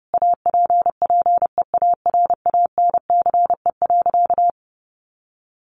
Morse code, heard as a keyed tone, about 28 wpm, 700 Hz. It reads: APPEARANCE.
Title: Sentences from Top 1100 Words 30wpm